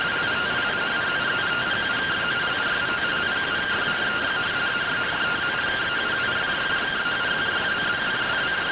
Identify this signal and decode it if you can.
Two-tone teleprinter signal, about 170 Hz shift.